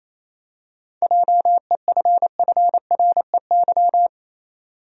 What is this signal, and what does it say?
Morse code, heard as a keyed tone, about 27 wpm, 700 Hz. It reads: JEFFREY
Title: USA Common Names - 28wpm